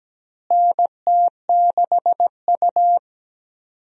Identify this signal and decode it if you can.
Morse code, heard as a keyed tone, about 17 wpm, 700 Hz. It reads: NT6U
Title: USA Call Signs - 4 Characters - Repeated 2x - 17wpm